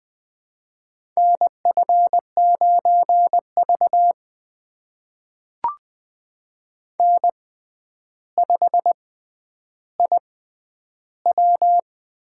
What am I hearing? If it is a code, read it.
Morse code, heard as a keyed tone, about 20 wpm, spaced as for 10 wpm, 700 Hz. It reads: NF9V N5IW
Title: USA Call Signs - 4 Characters - 6x Character Spacing - 20wpm